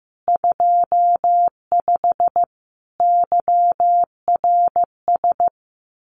Morse code, heard as a keyed tone, about 15 wpm, 700 Hz. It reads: 25 YRS
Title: Common QSO Phrases Encore 15wpm